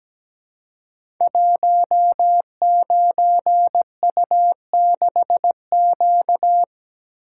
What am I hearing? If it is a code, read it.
Morse code, heard as a keyed tone, about 17 wpm, 700 Hz. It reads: 19U6Q
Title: Code Groups - 5 Letter:Numbers - 17wpm